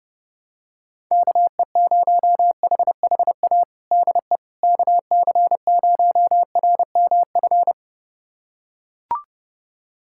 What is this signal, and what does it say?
Morse code, heard as a keyed tone, about 30 wpm, 700 Hz. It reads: KE0HHA DE KC0RMF
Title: USA Callsign DE Callsign 30wpm